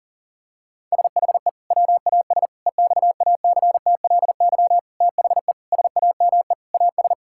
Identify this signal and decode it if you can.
Morse code, heard as a keyed tone, about 40 wpm, 700 Hz. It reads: SHE WAS EXACTLY THE SAME AS
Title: Sentences from Top 1000 Words - Review Part 6 - 40wpm